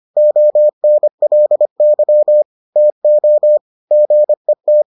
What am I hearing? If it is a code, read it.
Morse code, heard as a keyed tone, about 25 wpm, 600 Hz. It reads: ONLY TO GET